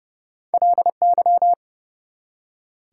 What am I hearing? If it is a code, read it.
Morse code, heard as a keyed tone, about 30 wpm, 700 Hz. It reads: LY
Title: Prefixes and Suffixes 30wpm